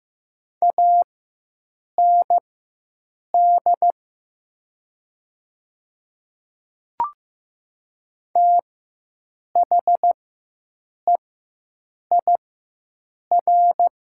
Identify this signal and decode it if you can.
Morse code, heard as a keyed tone, about 15 wpm, spaced as for 7 wpm, 700 Hz. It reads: AND THEIR
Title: 4x Character Spacing - Top 100 Words - 15wpm